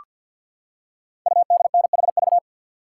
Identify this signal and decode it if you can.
Morse code, heard as a keyed tone, about 48 wpm, 690 Hz. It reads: UDNHV
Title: Code Groups - 5 Letters - 50wpm